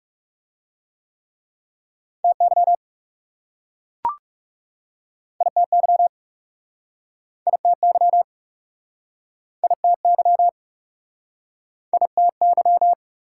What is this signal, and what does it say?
Morse code, heard as a keyed tone, about 40 wpm, 700 Hz. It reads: TY ITY ITY ITY ITY